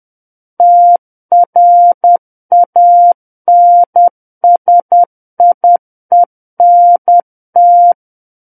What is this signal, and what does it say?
Morse code, heard as a keyed tone, about 10 wpm, 700 Hz. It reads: TRANSIENT
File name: Book One / 01-Chapter 1 - 10 WPM